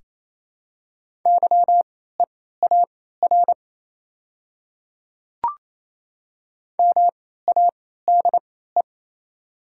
Morse code, heard as a keyed tone, about 28 wpm, spaced as for 16 wpm, 690 Hz. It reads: YEAR MADE
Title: 3x Character Spacing - Top 200 Words - 28wpm